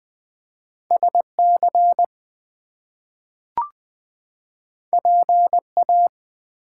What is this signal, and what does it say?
Morse code, heard as a keyed tone, about 20 wpm, 700 Hz. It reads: SC PA
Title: US States and Canadian Provinces 20wpm